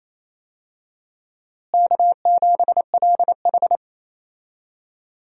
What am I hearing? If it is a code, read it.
Morse code, heard as a keyed tone, about 27 wpm, 700 Hz. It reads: K7LH
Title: USA Call Signs - 4 Characters - Repeated 2x - 28wpm